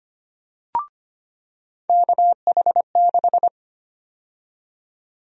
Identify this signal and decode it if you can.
Morse code, heard as a keyed tone, about 25 wpm, 700 Hz. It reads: KH6
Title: USA Call Sign Prefixes - 25wpm